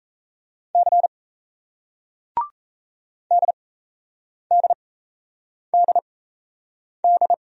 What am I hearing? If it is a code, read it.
Morse code, heard as a keyed tone, about 38 wpm, spaced as for 8 wpm, 700 Hz. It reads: C DDDD